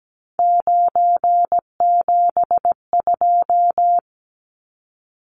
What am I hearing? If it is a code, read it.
Morse code, heard as a keyed tone, about 17 wpm, 700 Hz. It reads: 972
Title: Numbers - 3-Digit 17wpm